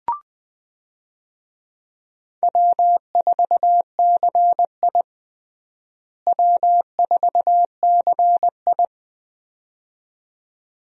Morse code, heard as a keyed tone, about 20 wpm, 700 Hz. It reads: W4CI W4CI
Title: USA Call Signs - 4 Characters - Repeated 2x - 20wpm